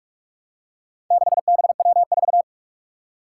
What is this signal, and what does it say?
Morse code, heard as a keyed tone, about 46 wpm, 700 Hz. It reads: 6BW4
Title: Four Challenging Letter-Number Combinations 45wpm